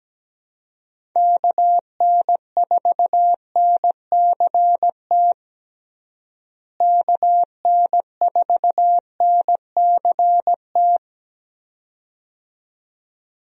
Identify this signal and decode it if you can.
Morse code, heard as a keyed tone, about 17 wpm, 700 Hz. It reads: KN4NCT KN4NCT
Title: USA Call Signs - 6 Characters - Repeated 2x - 17wpm